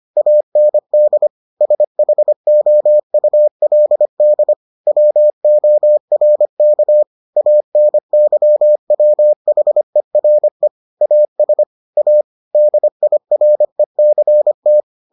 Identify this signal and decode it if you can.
Morse code, heard as a keyed tone, about 25 wpm, 600 Hz. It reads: AND SHOULD WORK ANYWHERE AS A DIRECT